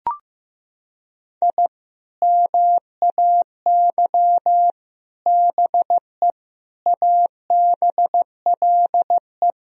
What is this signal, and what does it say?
Morse code, heard as a keyed tone, about 15 wpm, 700 Hz. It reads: I MAY BE ABLE
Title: Sentences from Top 1000 Words - Review Part 2 - 15wpm